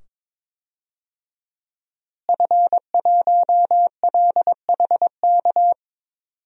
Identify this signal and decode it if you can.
Morse code, heard as a keyed tone, about 22 wpm, 700 Hz. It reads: F1LHK
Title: DXCC-100 Call Signs - 5 Characters - Repeated 2x - 22wpm